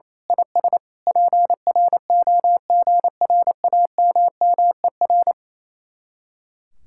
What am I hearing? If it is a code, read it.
Morse code, heard as a keyed tone, about 26 wpm, 700 Hz. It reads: IS PROGRAMMER
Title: Speed-Racing Common QSO Phrases Encore - 28wpm